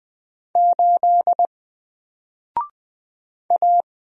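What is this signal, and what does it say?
Morse code, heard as a keyed tone, about 20 wpm, 700 Hz. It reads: 8 A
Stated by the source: Single letter and number Morse drill at 20 wpm, from N4LCD.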